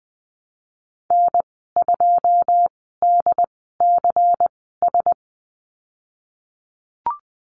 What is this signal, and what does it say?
Morse code, heard as a keyed tone, about 20 wpm, spaced as for 15 wpm, 700 Hz. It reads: N2DCS
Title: 122 Lesson 20g - Number 3 - Call Signs 20wpm